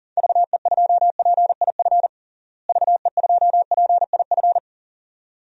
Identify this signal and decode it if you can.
Morse code, heard as a keyed tone, about 40 wpm, 700 Hz. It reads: VE2PIF VE2PIF
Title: DXCC-100 Call Signs - 6 Characters - Repeated 2x - 40wpm